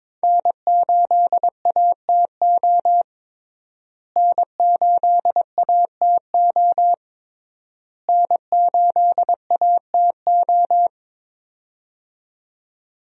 Morse code, heard as a keyed tone, about 22 wpm, 700 Hz. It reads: N8ATO N8ATO N8ATO
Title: USA Call Signs - 5 Characters - Repeated 3x - 22wpm